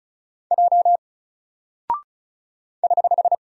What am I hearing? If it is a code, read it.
Morse code, heard as a keyed tone, about 36 wpm, 700 Hz. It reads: J <HH>
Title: Single Character (Letter:Number:Punctuation:Prosign) 35wpm